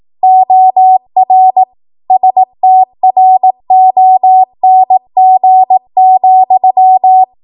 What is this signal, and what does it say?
Morse code, heard as a keyed tone, about 18 wpm, 750 Hz. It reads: OR STRONG,
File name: Morse / 18WPM